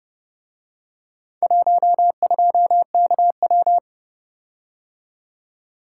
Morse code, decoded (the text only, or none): 12KW